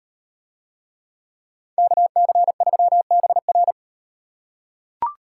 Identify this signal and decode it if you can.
Morse code, heard as a keyed tone, about 38 wpm, 700 Hz. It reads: KC3BR